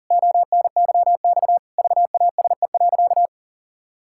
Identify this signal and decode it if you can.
Morse code, heard as a keyed tone, about 40 wpm, 700 Hz. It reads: ONYX VASE.
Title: Pangram Challenge v2 - 40wpm